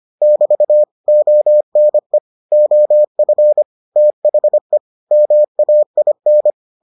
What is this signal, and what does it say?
Morse code, heard as a keyed tone, about 25 wpm, 600 Hz. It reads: <BT> ONE OF THE MAIN